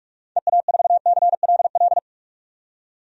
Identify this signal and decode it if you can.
Morse code, heard as a keyed tone, about 43 wpm, 700 Hz. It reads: EA4CLL
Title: DXCC-100 Call Signs - 6 Characters - 45wpm